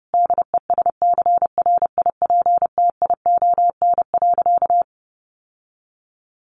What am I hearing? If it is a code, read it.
Morse code, heard as a keyed tone, about 30 wpm, 700 Hz. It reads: DESCRIPTION.
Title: Sentences from Top 1300 Words 30wpm